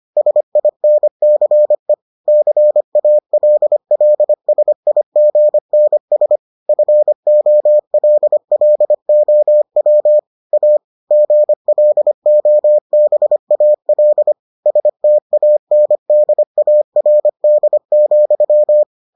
Morse code, heard as a keyed tone, about 25 wpm, 600 Hz. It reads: SINCE CALLSIGNS FOLLOW A GLOBAL STANDARD,